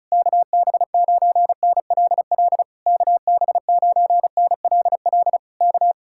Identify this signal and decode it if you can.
Morse code, heard as a keyed tone, about 37 wpm, 700 Hz. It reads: KB9NLL KB9NLL K
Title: Speed-Racing Common QSO Phrases Encore - 30wpm